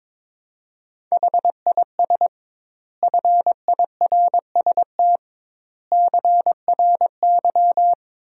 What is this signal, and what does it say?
Morse code, heard as a keyed tone, about 22 wpm, 700 Hz. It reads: HIS FIRST CRY
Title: 2x Word-Spacing - Sentences from Top 500 Words - 22wpm